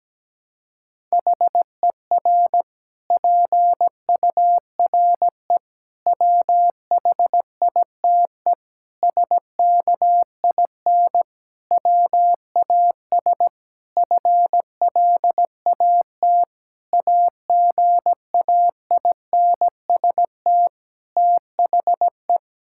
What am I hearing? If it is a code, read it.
Morse code, heard as a keyed tone, about 17 wpm, 700 Hz. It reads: HER PURE WHITE SKIN WAS FLAT AGAINST THE
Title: Sentences from Top 1600 Words 17wpm